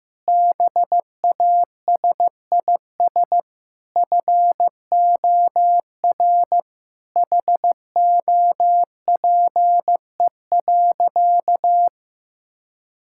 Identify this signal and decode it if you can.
Morse code, heard as a keyed tone, about 15 wpm, 700 Hz. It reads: BASIS FOR HOPE.
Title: Sentences from Top 1000 Words - Review Part 5 - 15wpm